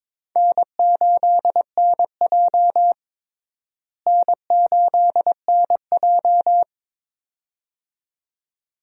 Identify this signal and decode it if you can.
Morse code, heard as a keyed tone, about 22 wpm, 700 Hz. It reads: N8NJ N8NJ
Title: USA Call Signs - 4 Characters - Repeated 2x - 22wpm